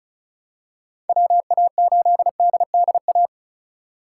Morse code, heard as a keyed tone, about 34 wpm, 700 Hz. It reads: WA8DDA
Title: USA Call Signs - 6 Characters - 2x Character Spacing - 35wpm